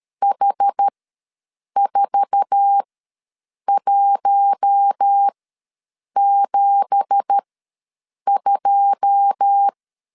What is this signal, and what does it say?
Morse code, heard as a keyed tone, about 13 wpm, spaced as for 7 wpm, 800 Hz. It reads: H4172
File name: LEZIONI / 9 2867 ws 12wpm 3431